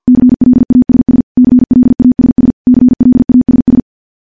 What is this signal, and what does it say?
Morse code, heard as a keyed tone, about 50 wpm, 250 Hz. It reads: CLASS CLASS CLASS
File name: morse / morse-training-50WPM